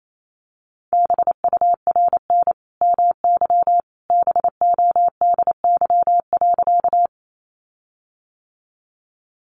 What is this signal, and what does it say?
Morse code, heard as a keyed tone, about 28 wpm, 700 Hz. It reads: BURN MY BODY.
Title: Sentences from Top 1300 Words 28wpm